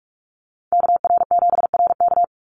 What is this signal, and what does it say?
Morse code, heard as a keyed tone, about 44 wpm, 700 Hz. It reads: KR7RK
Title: NA SOTA Activator Callsigns 45wpm